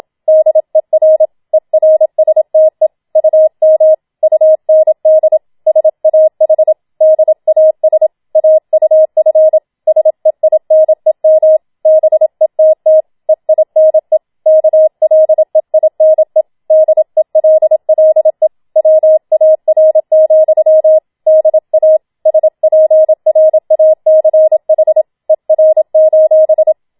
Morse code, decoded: DER ERSTE UM UND SAH DAS AUF SEINEM BETT EINE KLEINE DELLE WAR, DA SPRACH ER:

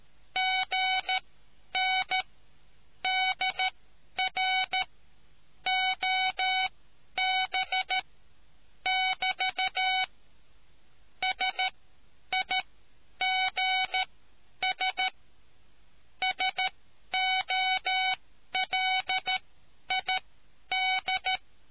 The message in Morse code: GN DR OB<BT> SIGS SOLID

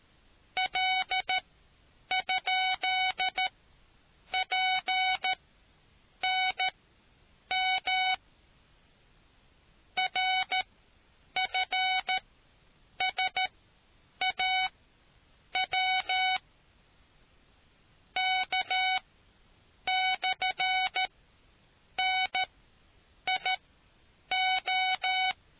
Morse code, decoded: L?PNM RFSAW K/NIO

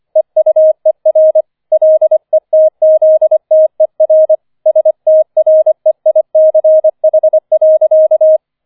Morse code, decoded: EUER LETZTER STREICH.

